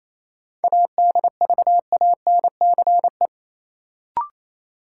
Morse code, heard as a keyed tone, about 28 wpm, 700 Hz. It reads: ADVANCE
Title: Sentences from Top 1400 Words - New Word Drill - 28wpm